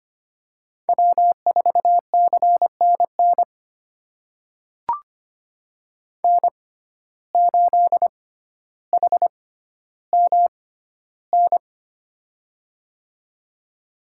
Morse code, heard as a keyed tone, about 25 wpm, spaced as for 13 wpm, 700 Hz. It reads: W4CNN N8HMN
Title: USA Call Signs - 5 Characters - 6x Character Spacing - 25wpm